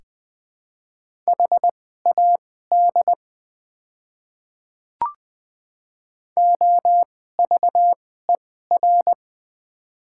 Morse code, heard as a keyed tone, about 20 wpm, spaced as for 15 wpm, 700 Hz. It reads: HAD OVER
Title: 2x Character Spacing - Top 200 Words - 20wpm